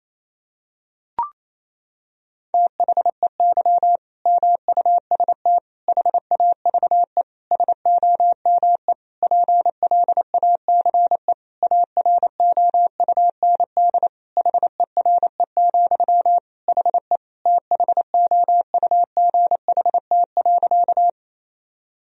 Morse code, heard as a keyed tone, about 28 wpm, 700 Hz. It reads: THEY MUST HAVE SOME PLACE AROUND HERE, HE THOUGHT.
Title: Sentences from Top 1000 Words - Review Part 3 - 28wpm